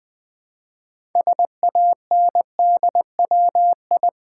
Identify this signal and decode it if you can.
Morse code, heard as a keyed tone, about 20 wpm, 700 Hz. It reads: SANDWI
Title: Build A Word - 2 Letters At A Time - 8-Letter Words - 20wpm